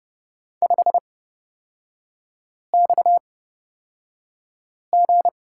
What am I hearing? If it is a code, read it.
Morse code, heard as a keyed tone, about 29 wpm, spaced as for 5 wpm, 690 Hz. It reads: 5XG
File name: Lesson 37e - Letter x - Three Characters 30wpm